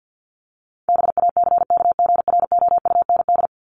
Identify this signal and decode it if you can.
Morse code, heard as a keyed tone, about 50 wpm, 700 Hz. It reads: BACKGROUND
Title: Top 1001-1250 Words 50wpm